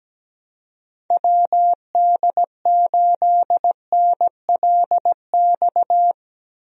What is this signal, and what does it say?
Morse code, heard as a keyed tone, about 17 wpm, 700 Hz. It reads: WD8NLX
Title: USA Call Signs - 6 Characters - Repeated 2x - 17wpm